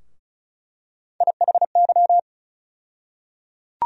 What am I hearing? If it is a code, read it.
Morse code, heard as a keyed tone, about 34 wpm, 700 Hz. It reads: IHY
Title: USA Call Sign Postfixes - 3 Letters 35wpm